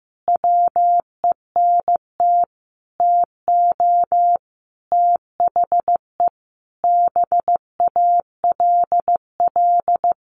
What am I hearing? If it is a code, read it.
Morse code, heard as a keyed tone, about 15 wpm, 700 Hz. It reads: WENT TO THE BALL
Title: Sentences from Top 900 Words 15wpm